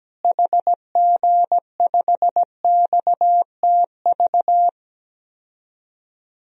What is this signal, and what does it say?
Morse code, heard as a keyed tone, about 17 wpm, 700 Hz. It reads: HG5XTV
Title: DXCC-100 Call Signs - 6 Characters - Repeated 2x - 17wpm